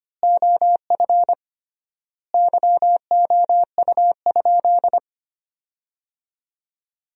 Morse code, heard as a keyed tone, about 25 wpm, 700 Hz. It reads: OF YOU?
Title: 3x Word-Spacing - Sentences from Top 300 Words - 25wpm